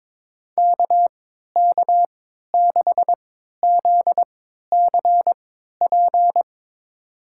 Morse code, heard as a keyed tone, about 22 wpm, spaced as for 13 wpm, 700 Hz. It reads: KK6ZCP
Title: USA Call Signs - 6 Characters - 3x Character Spacing - 22wpm